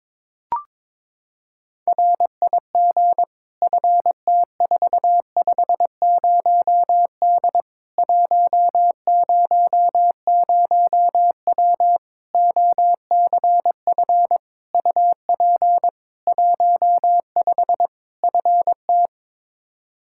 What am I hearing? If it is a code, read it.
Morse code, heard as a keyed tone, about 22 wpm, 700 Hz. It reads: RIG FT450D 100W OCF UP 15 FT
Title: Practice QSO - Exchange 2 - Info - 22wpm